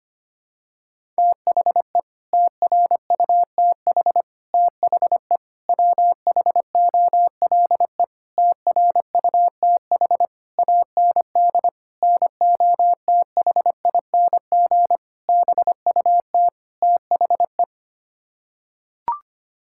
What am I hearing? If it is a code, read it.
Morse code, heard as a keyed tone, about 25 wpm, 700 Hz. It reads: THE TRUTH THE WHOLE TRUTH AND NOTHING BUT THE
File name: Sets of 9 Words - 25wpm